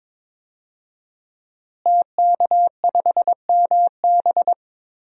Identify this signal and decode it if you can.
Morse code, heard as a keyed tone, about 22 wpm, 700 Hz. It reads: TK5MB
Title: DXCC-100 Call Signs - 5 Characters - Repeated 3x - 22wpm